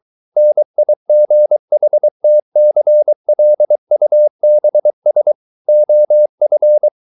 Morse code, heard as a keyed tone, about 23 wpm, 600 Hz. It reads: NIGHTCLUBS OF